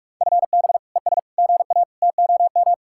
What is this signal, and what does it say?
Morse code, heard as a keyed tone, about 45 wpm, 700 Hz. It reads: FB ES GA TOM